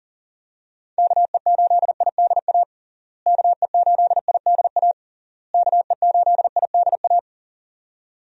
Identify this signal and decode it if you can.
Morse code, heard as a keyed tone, about 40 wpm, 700 Hz. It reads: KE8IDA KE8IDA KE8IDA
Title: USA Call Signs - 6 Characters - Repeated 3x - 40wpm